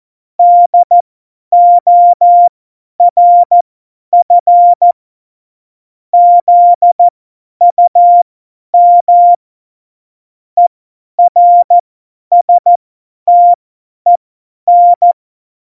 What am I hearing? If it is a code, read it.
Morse code, heard as a keyed tone, about 14 wpm, spaced as for 10 wpm, 700 Hz. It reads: DORF ZUM ERSTEN
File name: Morsekurs / Witze / Witz 11. September 2018, 50 F70